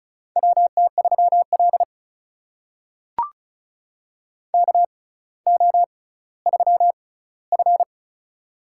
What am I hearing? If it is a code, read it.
Morse code, heard as a keyed tone, about 34 wpm, spaced as for 18 wpm, 700 Hz. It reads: WT3L KO3F
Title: USA Call Signs - 4 Characters - 6x Character Spacing - 35wpm